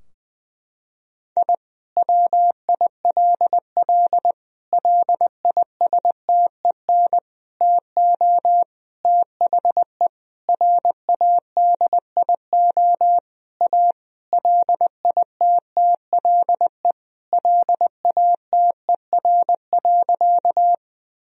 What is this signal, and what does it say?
Morse code, heard as a keyed tone, about 20 wpm, 700 Hz. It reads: I WILL LISTEN TO THE RADIO A LITTLE LATER.
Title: Sentences from Top 1000 Words - Review Part 3 - 20wpm